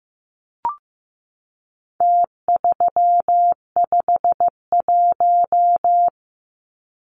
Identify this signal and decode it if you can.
Morse code, heard as a keyed tone, about 15 wpm, 700 Hz. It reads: T351
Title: Numbers - Contest Style - 4-Digit 15wpm